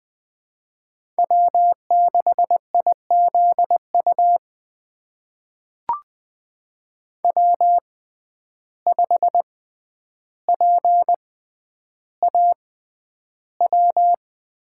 Morse code, heard as a keyed tone, about 20 wpm, spaced as for 10 wpm, 700 Hz. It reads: W6IZU W5PAW
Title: USA Call Signs - 5 Characters - 6x Character Spacing - 20wpm